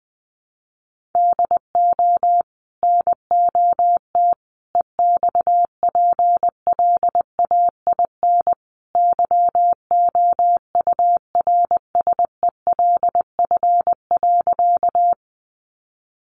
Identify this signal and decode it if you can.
Morse code, heard as a keyed tone, about 20 wpm, 700 Hz. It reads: DO NOT EXPLAIN YOURSELF.
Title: Sentences from Top 1100 Words 20wpm